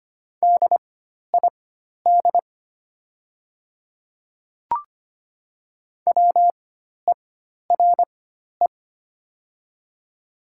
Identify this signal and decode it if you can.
Morse code, heard as a keyed tone, about 25 wpm, spaced as for 12 wpm, 700 Hz. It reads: DID WERE
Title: 4x Character Spacing - Top 100 Words - 25wpm